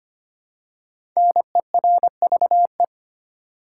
Morse code, heard as a keyed tone, about 25 wpm, 700 Hz. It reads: NERVE
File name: 5 Letter Words - First 3 Letters Given - 25wpm